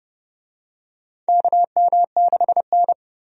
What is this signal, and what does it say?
Morse code, heard as a keyed tone, about 30 wpm, 700 Hz. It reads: KM6N
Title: USA Call Signs - 4 Characters - 30wpm